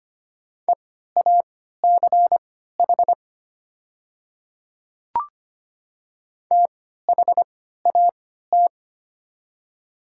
Morse code, heard as a keyed tone, about 25 wpm, spaced as for 14 wpm, 700 Hz. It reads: EACH THAT
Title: 3x Character Spacing - Top 100 Words - 25wpm